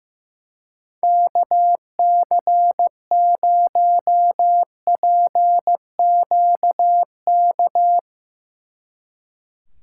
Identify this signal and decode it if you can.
Morse code, heard as a keyed tone, about 16 wpm, 700 Hz. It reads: KC0PQK
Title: Speed Racing - USA Call Signs 15wpm